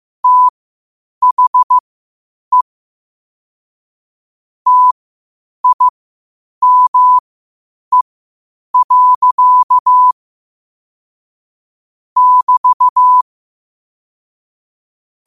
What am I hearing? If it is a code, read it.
Morse code, heard as a keyed tone, about 15 wpm, spaced as for 8 wpm, 1000 Hz. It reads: THE TIME. <BT>
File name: qotdfiles / 8wpm